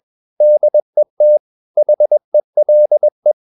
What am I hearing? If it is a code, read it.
Morse code, heard as a keyed tone, about 21 wpm, 600 Hz. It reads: DET HELE
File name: Ferie QRS 21wpm